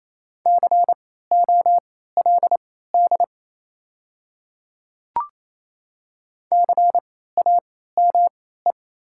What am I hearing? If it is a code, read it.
Morse code, heard as a keyed tone, about 28 wpm, spaced as for 16 wpm, 700 Hz. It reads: COLD CAME
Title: 3x Character Spacing - Top 500 Words - 28wpm